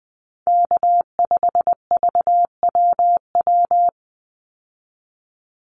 Morse code, heard as a keyed tone, about 20 wpm, 700 Hz. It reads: K5VWW
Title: Call Signs Repeated 2x - 20wpm